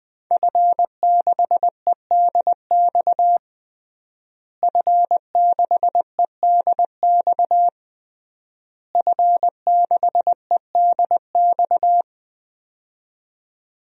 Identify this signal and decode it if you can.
Morse code, heard as a keyed tone, about 20 wpm, 700 Hz. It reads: F6EDX F6EDX F6EDX
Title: DXCC-100 Call Signs - 5 Characters - Repeated 3x - 20wpm